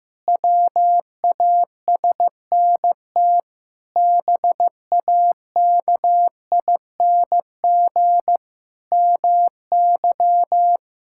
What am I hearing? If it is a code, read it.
Morse code, heard as a keyed tone, about 15 wpm, 700 Hz. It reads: WASNT BAKING MY
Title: Pangram Challenge v2 - 15wpm